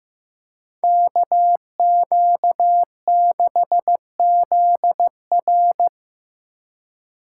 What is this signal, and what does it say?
Morse code, heard as a keyed tone, about 18 wpm, 700 Hz. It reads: KQ6ZR
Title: Speed Racing - USA Call Signs 15wpm